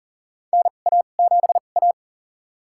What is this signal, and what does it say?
Morse code, heard as a keyed tone, about 41 wpm, spaced as for 29 wpm, 700 Hz. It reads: NA7A
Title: USA Call Signs - 4 Characters - 2x Character Spacing - 40wpm